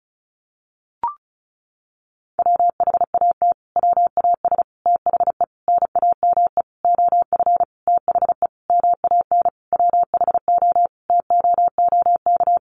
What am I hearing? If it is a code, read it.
Morse code, heard as a keyed tone, about 34 wpm, 700 Hz. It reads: WHAT WAS THE NAME OF THE MAN WHO TOOK
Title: Sentences from Top 300 Words 35wpm